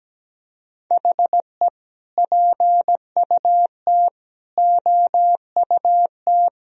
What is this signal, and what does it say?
Morse code, heard as a keyed tone, about 17 wpm, 700 Hz. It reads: HE PUT OUT
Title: Sentences from Top 1000 Words - Review Part 3 - 17wpm